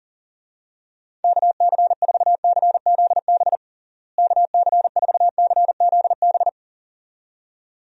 Morse code, heard as a keyed tone, about 40 wpm, 700 Hz. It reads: KC4CZB KC4CZB
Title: USA Call Signs - 6 Characters - Repeated 2x - 40wpm